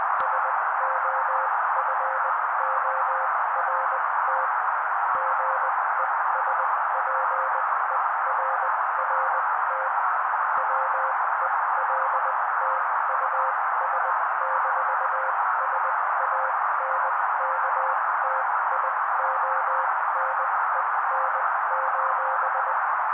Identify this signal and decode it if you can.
Morse code, heard as a keyed tone, about 20 wpm, spaced as for 16 wpm, 530 Hz. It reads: SOFORT GESPERRT WELT US-SANKTIONEN:
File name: daily / 16wpm noise